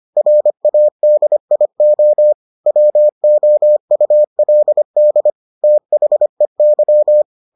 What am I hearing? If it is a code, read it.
Morse code, heard as a keyed tone, about 25 wpm, 600 Hz. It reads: RADIO WOULD THEY